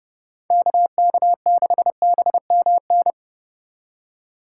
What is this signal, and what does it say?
Morse code, heard as a keyed tone, about 36 wpm, 700 Hz. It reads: KK6BMN
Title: Speed Racing - USA 2x3 Call Signs 30wpm